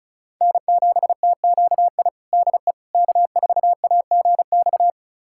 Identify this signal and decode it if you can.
Morse code, heard as a keyed tone, about 36 wpm, 700 Hz. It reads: N7TQI DE K4AGX
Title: USA Callsign DE Callsign 35wpm